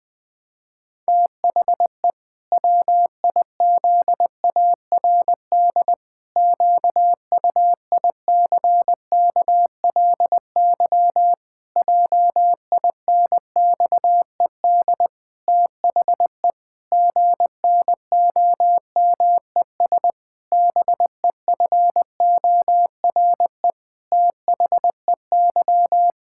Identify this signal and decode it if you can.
Morse code, heard as a keyed tone, about 20 wpm, 700 Hz. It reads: THE WIZARD QUICKLY JINXED THE GNOMES BEFORE THEY
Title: Pangram Challenge v2 - 20wpm